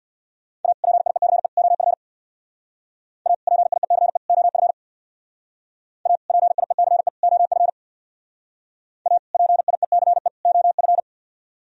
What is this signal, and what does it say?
Morse code, heard as a keyed tone, about 60 wpm, 700 Hz. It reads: A PIECE OF A PIECE OF A PIECE OF A PIECE OF
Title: Speed-Racing - Sets of 3 Words 50wpm